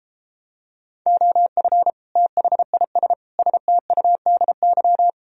Morse code, heard as a keyed tone, about 33 wpm, 700 Hz. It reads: OF THIS STUDY